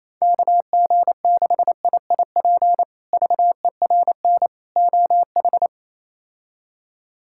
Morse code, heard as a keyed tone, about 28 wpm, 700 Hz. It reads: KG6IIP VERN OH
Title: K1USN SST QSO Practice - 28wpm